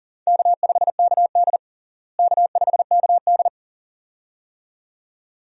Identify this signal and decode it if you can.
Morse code, heard as a keyed tone, about 40 wpm, 700 Hz. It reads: K5KD K5KD
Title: USA Call Signs - 4 Characters - Repeated 2x - 40wpm